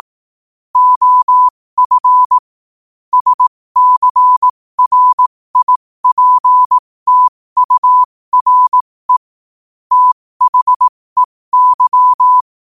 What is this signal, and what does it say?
Morse code, heard as a keyed tone, about 18 wpm, spaced as for 15 wpm, 1000 Hz. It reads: OF SCRIPTURE THEY
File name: qotdfiles / 15wpm